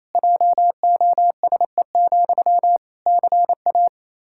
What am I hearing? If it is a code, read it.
Morse code, heard as a keyed tone, about 28 wpm, 700 Hz. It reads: JOSE, CA
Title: Practice QSO - Exchange 1 - Beginning - 28wpm